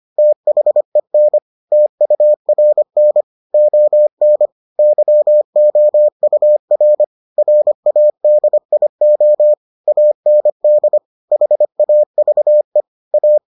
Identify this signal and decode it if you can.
Morse code, heard as a keyed tone, about 25 wpm, 600 Hz. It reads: THEN TURN ON YOUR RADIO AND HAVE A